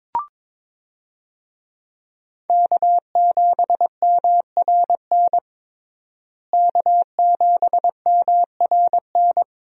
Morse code, heard as a keyed tone, about 22 wpm, 700 Hz. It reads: K7MRN K7MRN
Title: USA Call Signs - 5 Characters - Repeated 3x - 22wpm